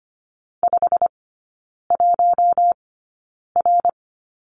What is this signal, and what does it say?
Morse code, heard as a keyed tone, about 25 wpm, spaced as for 9 wpm, 700 Hz. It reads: 51R
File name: course / 25:10wpm / 093 Lesson 16e - Letter c - Three Characters 25wpm